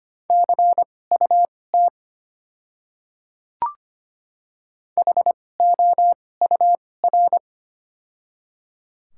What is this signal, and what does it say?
Morse code, heard as a keyed tone, about 25 wpm, spaced as for 18 wpm, 700 Hz. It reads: CUT HOUR
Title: 2x Character Spacing - Top 500 Words - 25wpm